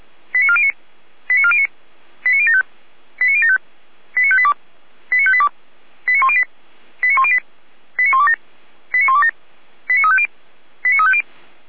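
Five-tone selective call (ZVEI), heard as quick runs of five tones, tones of about 70 ms. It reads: 89409 89409 89075 89075 89582 89582 89108 89108 89127 89127 89350 89350
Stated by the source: ZVEI five-tone selective call, code 89018, received by SDR.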